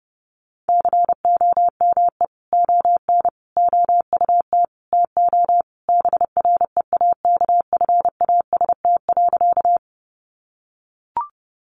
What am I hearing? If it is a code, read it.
Morse code, heard as a keyed tone, about 30 wpm, 700 Hz. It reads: COME ON OUT TO BREAKFAST.
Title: Sentences from Top 1500 Words 30wpm